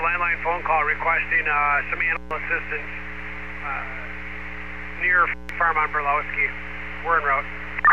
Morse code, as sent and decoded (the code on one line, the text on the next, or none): none